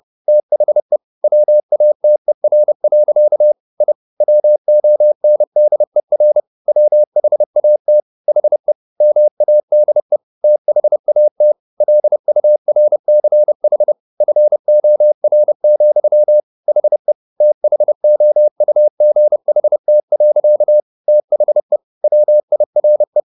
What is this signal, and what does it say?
Morse code, heard as a keyed tone, about 30 wpm, 600 Hz. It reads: THE WATER. I WONDER WHAT HE MADE THAT LURCH FOR, HE THOUGHT. THE WIRE
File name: the old man and the sea / the old man and the sea 1 30wpm